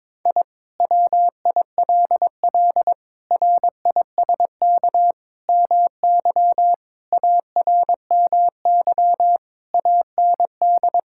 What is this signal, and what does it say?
Morse code, heard as a keyed tone, about 22 wpm, 700 Hz. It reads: I WILL RISK MY ARMY AND
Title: Sentences from Top 1000 Words - Review Part 5 - 22wpm